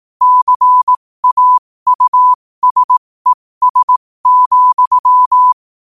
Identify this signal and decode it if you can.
Morse code, heard as a keyed tone, about 18 wpm, spaced as for 15 wpm, 1000 Hz. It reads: CAUSES,
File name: qotdfiles / 15wpm